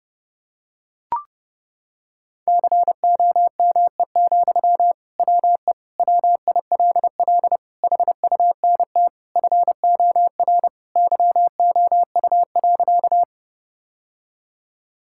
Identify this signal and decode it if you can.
Morse code, heard as a keyed tone, about 32 wpm, 700 Hz. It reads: COME, WE WILL HUNT FOR YOU.
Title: Sentences from Top 1600 Words 30wpm